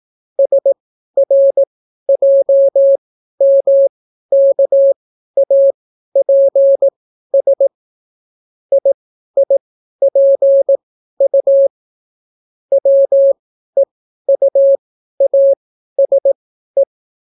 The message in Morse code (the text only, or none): SRJMKAPS IIPU WEUASE